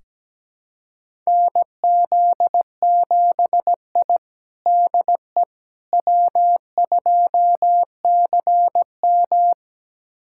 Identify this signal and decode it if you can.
Morse code, heard as a keyed tone, about 17 wpm, 700 Hz. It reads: NZ7I DE W2CM
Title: USA Callsign DE Callsign - 4 Characters - 17wpm